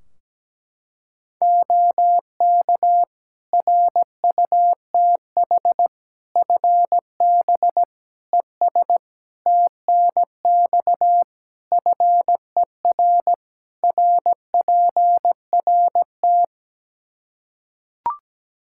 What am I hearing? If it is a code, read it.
Morse code, heard as a keyed tone, about 17 wpm, 700 Hz. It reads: OK RUTH FB ES TNX FER RPRT
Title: Practice QSO - Exchange 2 - Info - 17wpm